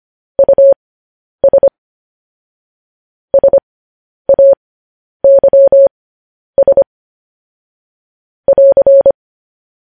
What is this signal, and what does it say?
Morse code, heard as a keyed tone, about 25 wpm, spaced as for 10 wpm, 550 Hz. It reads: US SAYS <AR>
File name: podcast / News headlines-1x-25x10